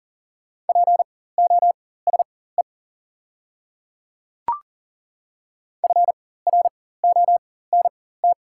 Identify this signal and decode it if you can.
Morse code, heard as a keyed tone, about 40 wpm, spaced as for 19 wpm, 690 Hz. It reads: POSE FRONT
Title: 4x Character Spacing - Top 500 Words - 40wpm